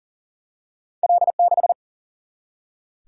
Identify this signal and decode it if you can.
Morse code, heard as a keyed tone, about 38 wpm, 700 Hz. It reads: L6
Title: Single-Letter Single-Number 40wpm